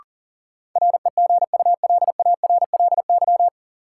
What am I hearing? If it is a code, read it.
Morse code, heard as a keyed tone, about 40 wpm, 700 Hz. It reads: REGULARLY
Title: Sentences from Top 1400 Words - New Word Drill Alt - 40wpm